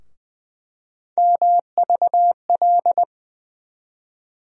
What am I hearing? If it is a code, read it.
Morse code, heard as a keyed tone, about 20 wpm, 700 Hz. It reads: MVL
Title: USA Call Sign Postfixes - 3 Letters 20wpm